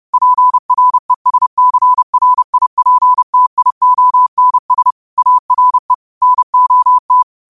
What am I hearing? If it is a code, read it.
Morse code, heard as a keyed tone, about 30 wpm, 1000 Hz. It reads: PRESCRIPTIONS ARE NOT
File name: qotdfiles / 30wpm